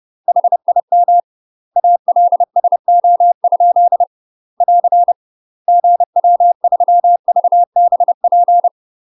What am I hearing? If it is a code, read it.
Morse code, heard as a keyed tone, about 30 wpm, 700 Hz. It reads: HIM ALSO? <AR> GW3VBP
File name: UK MASTER QSOs TRAINING / 30WPM / UK MASTER QSOs 0041